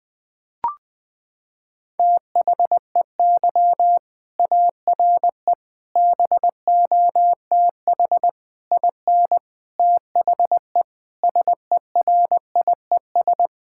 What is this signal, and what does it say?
Morse code, heard as a keyed tone, about 20 wpm, 700 Hz. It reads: THEY ARE BOTH IN THE SERIES
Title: Sentences from Top 1000 Words - Review Part 5 - 20wpm